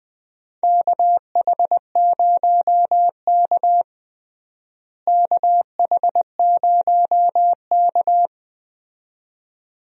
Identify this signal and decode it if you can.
Morse code, heard as a keyed tone, about 20 wpm, 700 Hz. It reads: KH0K KH0K
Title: USA Call Signs - 4 Characters - Repeated 2x - 20wpm